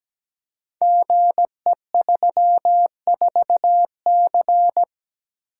Morse code, heard as a keyed tone, about 17 wpm, 700 Hz. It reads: GE34C
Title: Code Groups - 5 Letter:Numbers - 17wpm